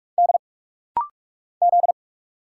Morse code, heard as a keyed tone, about 44 wpm, 700 Hz. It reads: D Z
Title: Single Letters - Rapid-Fire 45wpm